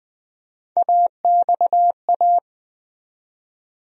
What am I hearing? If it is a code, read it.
Morse code, heard as a keyed tone, about 20 wpm, 700 Hz. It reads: AXA
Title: USA Call Sign Postfixes - 3 Letters 20wpm